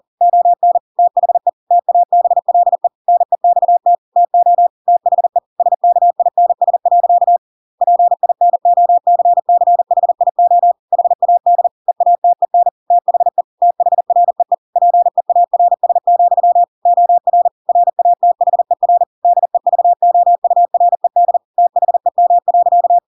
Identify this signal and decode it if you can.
Morse code, heard as a keyed tone, about 40 wpm, 700 Hz. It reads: ON THE TABLE NEXT TO THE SKINS. PINOCCHIO HAD EATEN THE THREE PEARS, OR RATHER DEVOURED THEM.
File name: Pinocchio / Pinocchio ch7 40 40 ik1hge